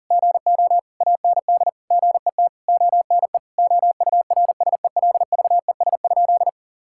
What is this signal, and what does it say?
Morse code, heard as a keyed tone, about 40 wpm, 700 Hz. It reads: GO AND GET ONE OURSELVES?
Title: Sentences from Top 1600 Words 40wpm